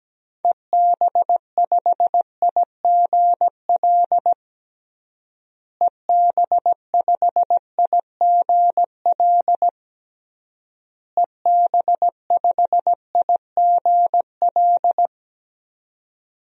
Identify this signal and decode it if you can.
Morse code, heard as a keyed tone, about 17 wpm, 700 Hz. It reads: EB5IGL EB5IGL EB5IGL
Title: DXCC-100 Call Signs - 6 Characters - Repeated 3x - 17wpm